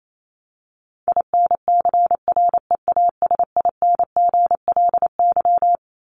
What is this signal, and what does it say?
Morse code, heard as a keyed tone, about 28 wpm, 700 Hz. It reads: INCREASINGLY
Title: LY Suffix - Top 100 Words with Base Words 28wpm